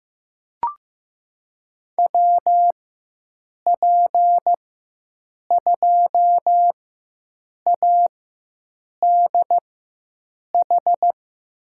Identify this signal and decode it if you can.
Morse code, heard as a keyed tone, about 15 wpm, spaced as for 7 wpm, 700 Hz. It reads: WP2ADH
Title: USA Call Signs - 6 Characters - 4x Character Spacing - 15wpm